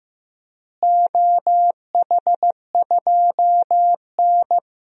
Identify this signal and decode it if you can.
Morse code, heard as a keyed tone, about 15 wpm, 700 Hz. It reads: OH2N
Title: DXCC-100 Call Signs - 4 Characters - Repeated 2x - 15wpm